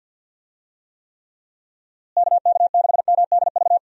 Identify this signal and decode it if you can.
Morse code, heard as a keyed tone, about 49 wpm, 700 Hz. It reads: KK6MDV
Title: USA Call Signs - 6 Characters - Repeated 3x - 50wpm